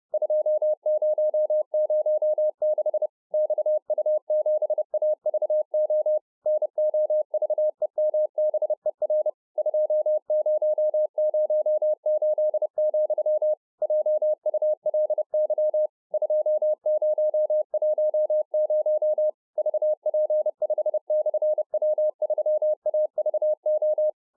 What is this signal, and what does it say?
Morse code, heard as a keyed tone, about 30 wpm, 610 Hz. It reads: 2006 XU7AVO NOVEMBER 2008, JULY 2010 VP5/W3AVO